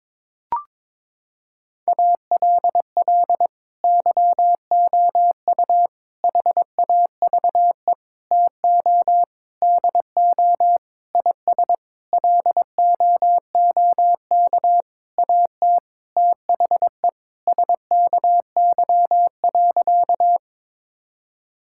Morse code, heard as a keyed tone, about 22 wpm, 700 Hz. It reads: ALL YOU HAVE TO DO IS LOOK AT THE SKY.
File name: Sentences with 9-Word Phrases - 22wpm